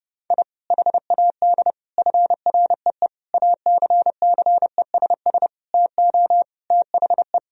Morse code, heard as a keyed tone, about 30 wpm, 700 Hz. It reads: I HAD FREE ACCESS TO THE
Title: Sentences from Top 1000 Words - Review Part 4 - 30wpm